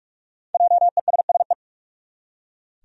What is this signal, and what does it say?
Morse code, heard as a keyed tone, about 45 wpm, 700 Hz. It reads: JESSE
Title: USA Common Names - Max 5 Letters - 45wpm